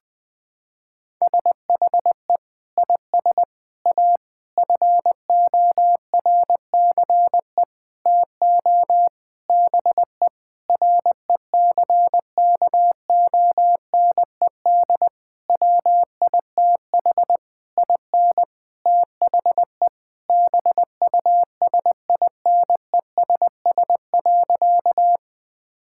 Morse code, heard as a keyed tone, about 20 wpm, 700 Hz. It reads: SHE IS A FORCE TO BE RECKONED WITH IN THE BUSINESS.
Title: Sentences with 8-Word Phrases - 20wpm